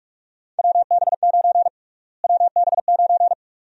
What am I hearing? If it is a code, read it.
Morse code, heard as a keyed tone, about 46 wpm, 700 Hz. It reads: WB9 WB9
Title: USA Call Sign Prefixes - Repeated 2x - 45wpm